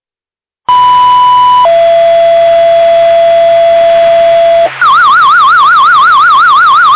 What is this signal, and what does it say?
Two-tone page, heard as a one-second tone, then a three-second tone, about 990 then 670 Hz, with noise. It is heard from 0.7 to 4.7 s.
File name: McAfee FD